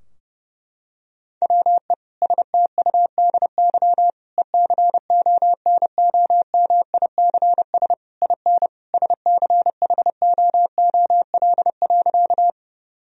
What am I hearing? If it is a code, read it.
Morse code, heard as a keyed tone, about 32 wpm, 700 Hz. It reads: WE STUDY ECONOMICS IN SCHOOL.
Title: Sentences from Top 1000 Words - Review Part 3 - 30wpm